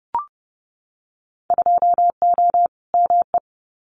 Morse code, heard as a keyed tone, about 30 wpm, 700 Hz. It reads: 2O ME
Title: Winter Field Day Exchanges Encore 30wpm